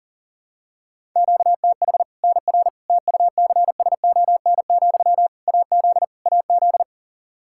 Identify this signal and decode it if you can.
Morse code, heard as a keyed tone, about 40 wpm, 700 Hz. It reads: QTH NR TUCSON, AZ AZ
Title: Practice QSO - Exchange 1 - Beginning - 40wpm